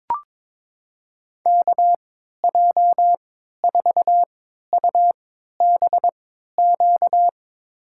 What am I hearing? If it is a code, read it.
Morse code, heard as a keyed tone, about 22 wpm, spaced as for 13 wpm, 700 Hz. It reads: KJ4UBQ
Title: USA Call Signs - 6 Characters - 3x Character Spacing - 22wpm